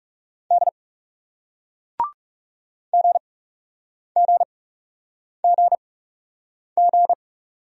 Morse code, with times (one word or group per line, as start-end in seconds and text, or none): 0.5-0.7 D
2.9-7.1 GGGG